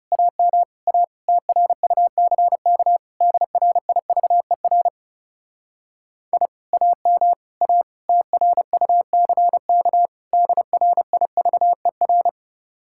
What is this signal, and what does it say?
Morse code, heard as a keyed tone, about 38 wpm, 700 Hz. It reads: AM A TRUCK DRIVER I AM A TRUCK DRIVER